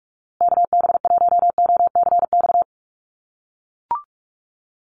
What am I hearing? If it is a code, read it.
Morse code, heard as a keyed tone, about 44 wpm, 700 Hz. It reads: KB1OCX
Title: USA Call Signs Repeated 3x 45wpm